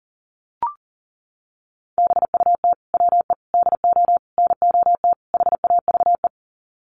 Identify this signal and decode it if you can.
Morse code, heard as a keyed tone, about 40 wpm, 700 Hz. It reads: BUT WE DO NOT HAVE
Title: Sets of 5 Words Encore2 40wpm